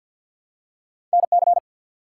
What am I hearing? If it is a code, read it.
Morse code, heard as a keyed tone, about 48 wpm, 700 Hz. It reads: NC
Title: US States and Canadian Provinces 50wpm